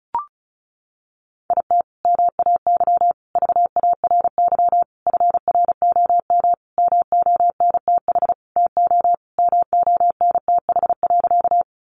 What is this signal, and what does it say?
Morse code, heard as a keyed tone, about 34 wpm, 700 Hz. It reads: IT MAY VARY FROM MONTH TO MONTH.
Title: Sentences from Top 1200 Words 35wpm